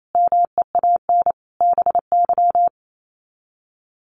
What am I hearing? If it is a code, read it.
Morse code, heard as a keyed tone, about 28 wpm, 700 Hz. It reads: MEAN BY
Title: Sets of 5 Words 28wpm